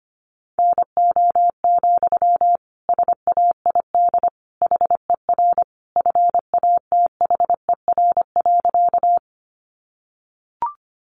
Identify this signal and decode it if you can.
Morse code, heard as a keyed tone, about 25 wpm, 700 Hz. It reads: NO, SAID HER FATHER.
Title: Sentences from Top 300 Words 25wpm